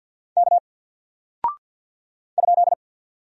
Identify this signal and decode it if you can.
Morse code, heard as a keyed tone, about 48 wpm, 700 Hz. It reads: K ?
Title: Single Character (Letter:Number:Punctuation:Prosign) Alt 50wpm